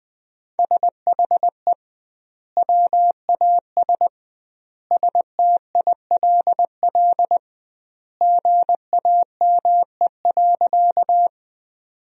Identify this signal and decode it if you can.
Morse code, heard as a keyed tone, about 21 wpm, 700 Hz. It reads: SHE WAS STILL GAME.
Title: 2x Word-Spacing - Sentences from Top 500 Words - 20wpm